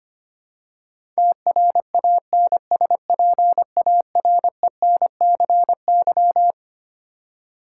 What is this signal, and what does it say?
Morse code, heard as a keyed tone, about 25 wpm, 700 Hz. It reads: TRANSPARENCY
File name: Twelve Letter Words - 25wpm